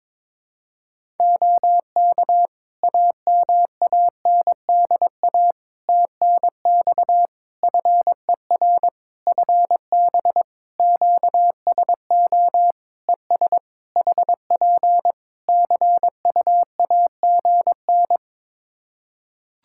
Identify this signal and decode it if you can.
Morse code, heard as a keyed tone, about 22 wpm, 700 Hz. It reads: OK AMANDA TNX FER FB QSO ES HP CUAGN
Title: Practice QSO - Exchange 4 - Ending - 22wpm